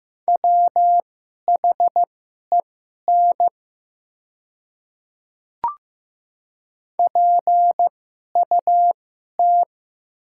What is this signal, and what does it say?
Morse code, heard as a keyed tone, about 15 wpm, spaced as for 11 wpm, 700 Hz. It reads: WHEN PUT
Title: 2x Character Spacing - Top 200 Words - 15wpm